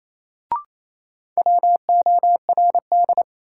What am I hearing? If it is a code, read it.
Morse code, heard as a keyed tone, about 28 wpm, 700 Hz. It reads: WORD
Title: Sentences from Top 300 Words - New Word Drill - 28wpm